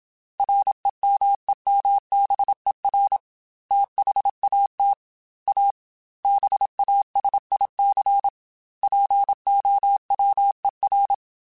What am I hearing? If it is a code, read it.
Morse code, heard as a keyed tone, about 27 wpm, 800 Hz. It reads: REMEMBER THAT A BASIC POWER
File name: Technical Subjects / Technical Subjects-mp3s / Sw Pwr Sup @25